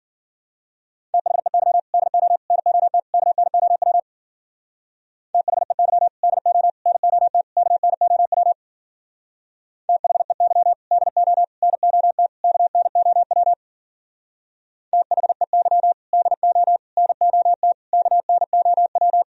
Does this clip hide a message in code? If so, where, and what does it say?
Yes: Morse code from 1.1 to 19.2 s, heard as a keyed tone, about 51 wpm, 700 Hz. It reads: THEY DO NOT KNOW THEY DO NOT KNOW THEY DO NOT KNOW THEY DO NOT KNOW